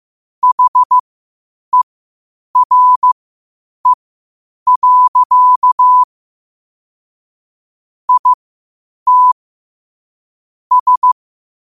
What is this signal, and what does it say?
Morse code, heard as a keyed tone, about 15 wpm, spaced as for 8 wpm, 1000 Hz. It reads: HERE. IT S
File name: qotdfiles / 8wpm